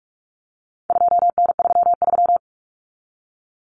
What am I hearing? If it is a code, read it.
Morse code, heard as a keyed tone, about 44 wpm, 700 Hz. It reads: 2N33
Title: Common Cut Numbers - 4-Digit 45wpm